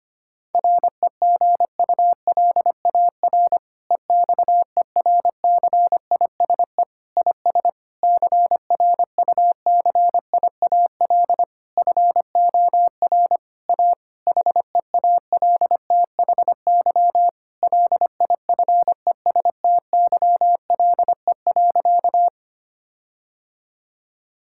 Morse code, heard as a keyed tone, about 25 wpm, 700 Hz. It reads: REGULAR EXERCISE IS CRUCIAL FOR A HEALTHY LIFESTYLE.
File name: practice / 25wpm / Difficult Sentences with Hints - 25wpm